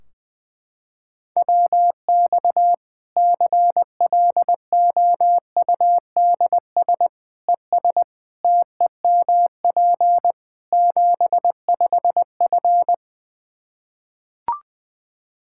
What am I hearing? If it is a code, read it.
Morse code, heard as a keyed tone, about 21 wpm, 700 Hz. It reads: WX CLOUDS ES TEMP 75F
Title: K1USN SST QSO Practice 20wpm